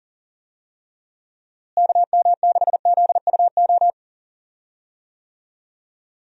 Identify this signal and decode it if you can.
Morse code, heard as a keyed tone, about 40 wpm, 700 Hz. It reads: KM6ZUO
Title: USA Call Signs - 6 Characters - Repeated 2x - 40wpm